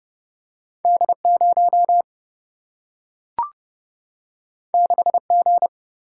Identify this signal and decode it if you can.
Morse code, heard as a keyed tone, about 30 wpm, 700 Hz. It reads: D0 6G
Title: Single-Letter Single-Number 30wpm